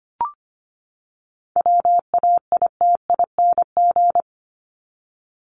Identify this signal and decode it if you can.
Morse code, heard as a keyed tone, about 25 wpm, 700 Hz. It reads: WAITING
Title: ING Suffix - Top 100 Words with Base Words 25wpm